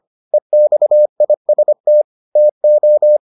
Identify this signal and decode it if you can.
Morse code, heard as a keyed tone, about 25 wpm, 600 Hz. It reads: EXIST TO